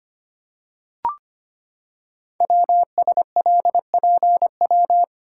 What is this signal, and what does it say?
Morse code, heard as a keyed tone, about 25 wpm, 700 Hz. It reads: WSLPW
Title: Code Groups - 5 Letters - 25wpm